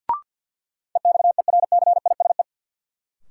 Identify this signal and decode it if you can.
Morse code, heard as a keyed tone, about 47 wpm, 700 Hz. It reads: EXERCISE
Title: Sentences from Top 1000 Words - Review Part 4 - New Word Drill - 50wpm